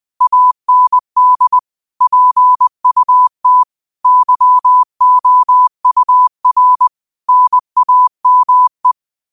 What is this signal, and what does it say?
Morse code, heard as a keyed tone, about 20 wpm, 1000 Hz. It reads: AND PUT YOUR NAME